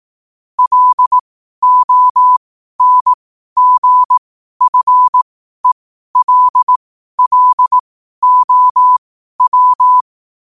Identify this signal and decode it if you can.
Morse code, heard as a keyed tone, about 18 wpm, spaced as for 12 wpm, 1000 Hz. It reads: LONGFELLOW
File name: qotdfiles / 12wpm